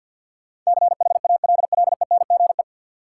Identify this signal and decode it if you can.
Morse code, heard as a keyed tone, about 53 wpm, 700 Hz. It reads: CHALLENGE
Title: Sentences from Top 1000 Words - Review Part 7 - New Word Drill - 50wpm